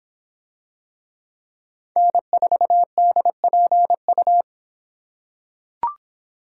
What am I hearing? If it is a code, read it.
Morse code, heard as a keyed tone, about 25 wpm, 700 Hz. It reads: N4DPU